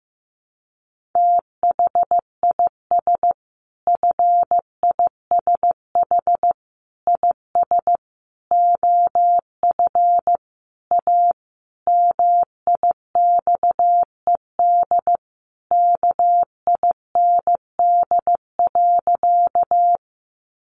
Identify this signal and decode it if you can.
Morse code, heard as a keyed tone, about 14 wpm, 700 Hz. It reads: THIS FISH IS OF A MIXED KIND.
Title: Sentences from Top 1500 Words 15wpm